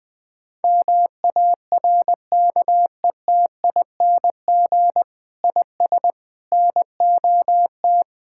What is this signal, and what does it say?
Morse code, heard as a keyed tone, about 21 wpm, 700 Hz. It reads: MARKETING IS NOT
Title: Sentences from Top 1000 Words - Review Part 5 - 20wpm